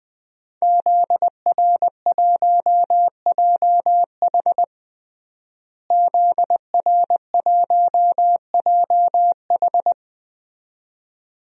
Morse code, heard as a keyed tone, about 20 wpm, 700 Hz. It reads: ZR1JH ZR1JH
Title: DXCC-100 Call Signs - 5 Characters - Repeated 2x - 20wpm